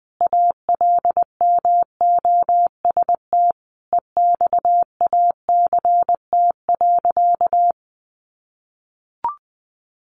Morse code, heard as a keyed tone, about 20 wpm, 700 Hz. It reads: ALMOST EXACT.
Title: Sentences from Top 1300 Words 20wpm